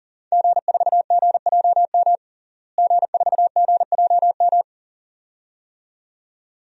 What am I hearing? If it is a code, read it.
Morse code, heard as a keyed tone, about 40 wpm, 700 Hz. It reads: G4GJM G4GJM
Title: DXCC-100 Call Signs - 5 Characters - Repeated 2x - 40wpm